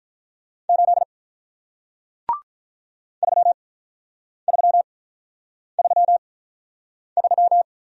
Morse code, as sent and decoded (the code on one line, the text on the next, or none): ---.. / ...-- ...-- ...-- ...--
8 3333